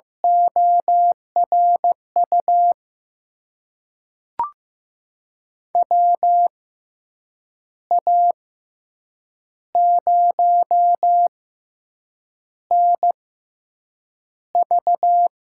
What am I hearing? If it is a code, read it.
Morse code, heard as a keyed tone, about 16 wpm, spaced as for 7 wpm, 700 Hz. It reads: ORU WA0NV
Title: USA Call Signs - 6 Characters - 6x Character Spacing - 15wpm